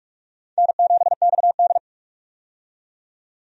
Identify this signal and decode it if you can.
Morse code, heard as a keyed tone, about 46 wpm, 700 Hz. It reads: N7XD
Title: USA Call Signs - 4 Characters - 45wpm